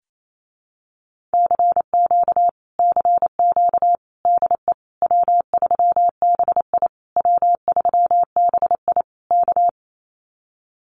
Morse code, heard as a keyed tone, about 28 wpm, 700 Hz. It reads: CQ CQ DE W3BI W3BI K
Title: Common QSO Phrases Encore 28wpm